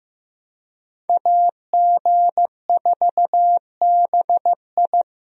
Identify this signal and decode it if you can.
Morse code, heard as a keyed tone, about 15 wpm, 700 Hz. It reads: AG4BI
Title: USA Call Signs - 5 Characters - Repeated 3x - 15wpm